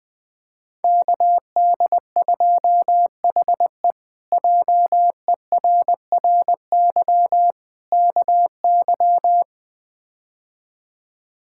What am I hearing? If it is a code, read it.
Morse code, heard as a keyed tone, about 21 wpm, 700 Hz. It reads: KD2HE JERRY KY
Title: K1USN SST QSO Practice - 20wpm